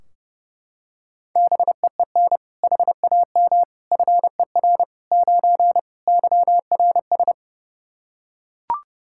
Morse code, heard as a keyed tone, about 32 wpm, 700 Hz. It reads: BEEN HAM FER 9 YRS
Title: Practice QSO - Exchange 3 - More Info - 30wpm